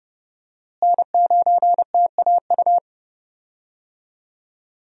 Morse code, heard as a keyed tone, about 30 wpm, 700 Hz. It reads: N9TAU
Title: USA Call Signs - 5 Characters - Repeated 2x - 30wpm